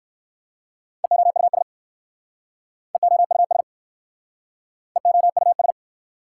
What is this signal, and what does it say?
Morse code, heard as a keyed tone, about 61 wpm, 700 Hz. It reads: EOUS EOUS EOUS